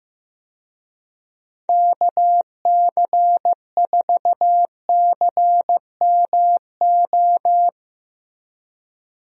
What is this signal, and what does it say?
Morse code, heard as a keyed tone, about 15 wpm, 700 Hz. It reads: KC4CMO
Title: USA Call Signs - 6 Characters - Repeated 3x - 15wpm